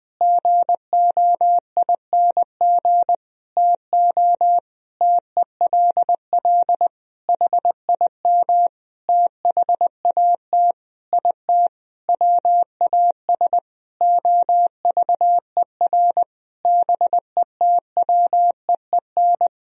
Morse code, heard as a keyed tone, about 20 wpm, 700 Hz. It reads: GOING TO TELL HIM THAT IT WAS OVER BETWEEN
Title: Sentences from Top 1000 Words - Review Part 1 - 20wpm